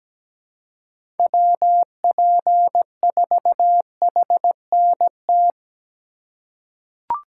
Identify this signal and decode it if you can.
Morse code, heard as a keyed tone, about 17 wpm, 700 Hz. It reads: WP4HNT
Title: USA Call Signs - 6 Characters - 2x Character Spacing - 17wpm